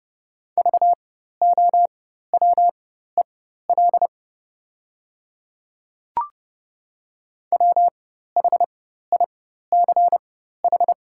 Morse code, heard as a keyed tone, about 30 wpm, spaced as for 14 wpm, 700 Hz. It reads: VOWEL WHICH
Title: 4x Character Spacing - Top 500 Words - 30wpm